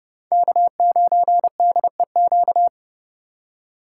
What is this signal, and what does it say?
Morse code, heard as a keyed tone, about 30 wpm, 700 Hz. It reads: K9DEQ
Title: USA Call Signs - 5 Characters - 2x Character Spacing - 30wpm